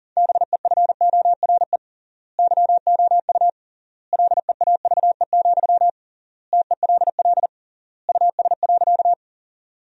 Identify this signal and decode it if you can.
Morse code, heard as a keyed tone, about 40 wpm, 700 Hz. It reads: BEFORE YOU LEAVE, TELL US.
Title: 3x Word-Spacing - Sentences from Top 400 Words - 40wpm